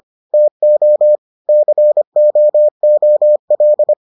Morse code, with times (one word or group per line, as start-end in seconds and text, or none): 0.3-1.2 TO
1.5-3.9 COOL